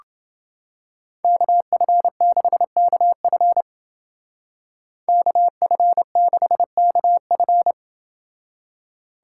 Morse code, heard as a keyed tone, about 29 wpm, 700 Hz. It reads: KF6KF KF6KF